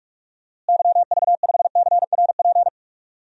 Morse code, heard as a keyed tone, about 41 wpm, 700 Hz. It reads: YV5CRP
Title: DXCC-100 Call Signs - 6 Characters - Repeated 3x - 45wpm